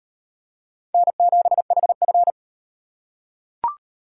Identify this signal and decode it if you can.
Morse code, heard as a keyed tone, about 33 wpm, 700 Hz. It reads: N7HF